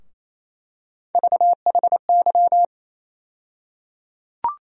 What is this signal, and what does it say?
Morse code, heard as a keyed tone, about 28 wpm, 700 Hz. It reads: VHY
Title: USA Call Sign Postfixes - 3 Letters 28wpm